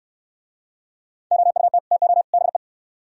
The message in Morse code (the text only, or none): OUT TO BE